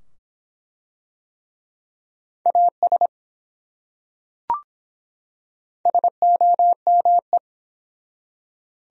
Morse code, heard as a keyed tone, about 23 wpm, 700 Hz. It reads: AS SOME